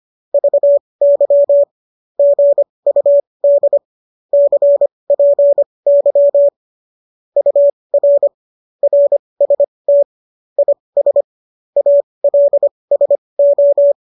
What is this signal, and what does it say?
Morse code, heard as a keyed tone, about 25 wpm, spaced as for 20 wpm, 570 Hz. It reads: VY GUD CPY UR RST IS ALSO